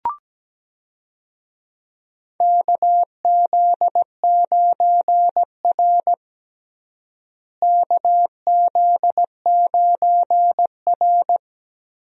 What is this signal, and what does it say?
Morse code, heard as a keyed tone, about 17 wpm, 700 Hz. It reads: KZ9R KZ9R
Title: USA Call Signs - 4 Characters - Repeated 3x - 17wpm